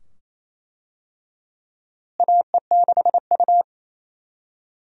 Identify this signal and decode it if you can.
Morse code, heard as a keyed tone, about 28 wpm, 700 Hz. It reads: AE6U
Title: USA Call Signs - 4 Characters - Repeated 2x - 28wpm